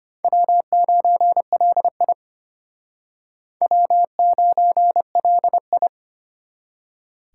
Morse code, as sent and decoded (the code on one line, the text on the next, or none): .-- ----. .-.. .. / .-- ----. .-.. ..
W9LI W9LI